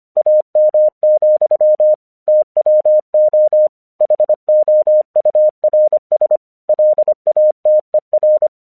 Morse code, heard as a keyed tone, about 25 wpm, 610 Hz. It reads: AM, TWO HOURS LATER
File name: English 25wpm